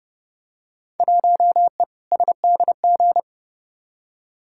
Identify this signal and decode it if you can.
Morse code, heard as a keyed tone, about 30 wpm, 700 Hz. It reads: 1E SDG
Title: Field Day Exchanges - 30wpm